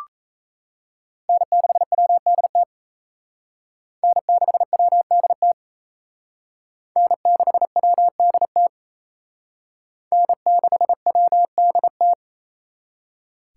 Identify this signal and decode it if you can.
Morse code, heard as a keyed tone, about 34 wpm, 700 Hz. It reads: N6WDT N6WDT N6WDT N6WDT
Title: Speed Racing - USA 1x3 Call Signs 28wpm